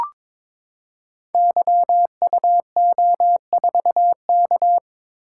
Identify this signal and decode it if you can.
Morse code, heard as a keyed tone, about 22 wpm, 700 Hz. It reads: YUO4K
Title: Code Groups - 5 Letter:Numbers - 22wpm